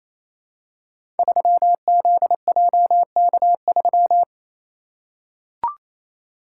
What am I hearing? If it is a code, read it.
Morse code, heard as a keyed tone, about 28 wpm, 700 Hz. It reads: 3ZJK3
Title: Code Groups - 5 Letter:Numbers - 28wpm